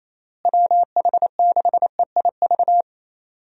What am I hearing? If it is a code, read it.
Morse code, heard as a keyed tone, about 28 wpm, 700 Hz. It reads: WH6EIV
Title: USA Call Signs - 6 Characters - Repeated 3x - 28wpm